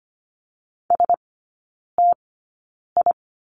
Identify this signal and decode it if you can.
Morse code, heard as a keyed tone, about 25 wpm, spaced as for 9 wpm, 690 Hz. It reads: STI
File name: Lesson 14e - Number 2 - Three Characters 25wpm